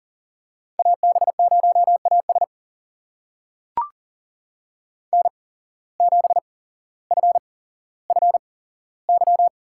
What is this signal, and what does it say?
Morse code, heard as a keyed tone, about 39 wpm, spaced as for 17 wpm, 700 Hz. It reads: AB0AS N7FFY
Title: USA Call Signs - 5 Characters - 8x Character Spacing - 40wpm